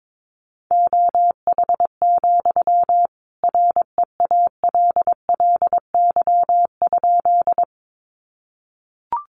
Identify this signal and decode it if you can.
Morse code, heard as a keyed tone, about 22 wpm, 700 Hz. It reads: OH, REALLY?
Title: Sentences from Top 600 Words 22wpm